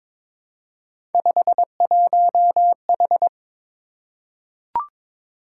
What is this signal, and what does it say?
Morse code, heard as a keyed tone, about 22 wpm, 700 Hz. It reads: 51H
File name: practice / 22wpm / Three Letter-Number Combinations - 22wpm